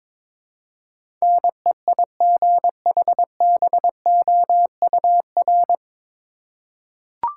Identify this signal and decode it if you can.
Morse code, heard as a keyed tone, about 22 wpm, 700 Hz. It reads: NEIGHBOUR
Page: Morse Code Ninja practice 22wpm